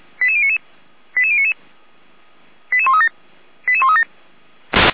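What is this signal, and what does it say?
Five-tone selective call (ZVEI), heard as quick runs of five tones, tones of about 70 ms. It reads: 88099 88099 88137 88137
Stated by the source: ZVEI five-tone selective call received on an SDR, from the Feuerwehr Orlamünde fire brigade audio server.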